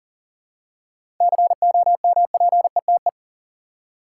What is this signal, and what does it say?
Morse code, heard as a keyed tone, about 40 wpm, 700 Hz. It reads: COMPETE
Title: Seven Letter Words - 40wpm